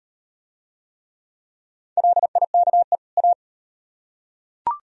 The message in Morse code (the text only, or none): LIKE A